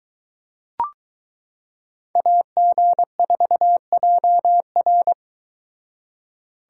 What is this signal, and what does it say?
Morse code, heard as a keyed tone, about 23 wpm, 700 Hz. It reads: AG4JR